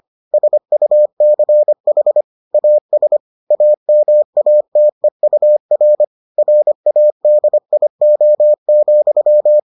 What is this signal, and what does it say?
Morse code, heard as a keyed tone, about 25 wpm, 600 Hz. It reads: SUCH AS AMATEUR RADIO,